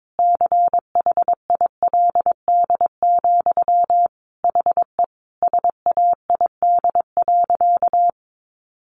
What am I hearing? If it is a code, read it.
Morse code, heard as a keyed tone, about 22 wpm, 700 Hz. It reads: CHILD, HE SAID.
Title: Sentences from Top 800 Words 22wpm